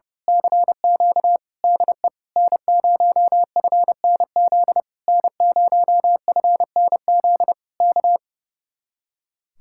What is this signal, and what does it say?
Morse code, heard as a keyed tone, about 29 wpm, 700 Hz. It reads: CQ DE N0FNZ N0FNZ K
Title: Speed-Racing Common QSO Phrases - 30wpm